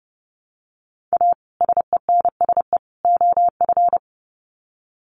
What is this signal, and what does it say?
Morse code, heard as a keyed tone, about 29 wpm, 700 Hz. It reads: A SENSE OF
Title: Sets of 3 Words 30wpm